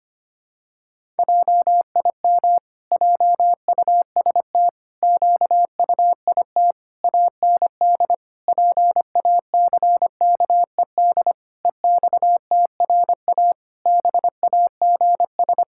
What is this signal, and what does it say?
Morse code, heard as a keyed tone, about 25 wpm, 700 Hz. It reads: JIM JUST QUIT AND PACKED EXTRA BAGS
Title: Pangram Challenge v2 - 25wpm